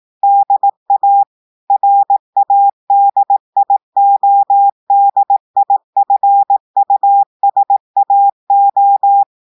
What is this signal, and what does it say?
Morse code, heard as a keyed tone, about 18 wpm, 800 Hz. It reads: DA RADIODIFUSAO
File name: audios / portugais / portugais texte en clair 1 vitesse 18 mots minutes